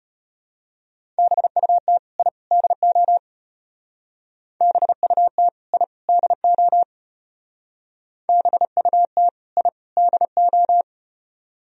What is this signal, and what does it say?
Morse code, heard as a keyed tone, about 32 wpm, 700 Hz. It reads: BUT I DO BUT I DO BUT I DO